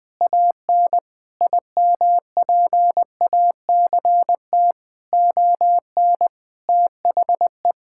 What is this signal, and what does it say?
Morse code, heard as a keyed tone, about 21 wpm, 700 Hz. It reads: AN IMPACT ON THE
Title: Sentences from Top 1000 Words - Review Part 6 - 20wpm